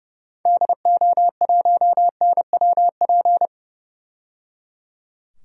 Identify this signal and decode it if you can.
Morse code, heard as a keyed tone, about 30 wpm, 700 Hz. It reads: DO1NWP
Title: DXCC-100 Call Signs - 6 Characters - 30wpm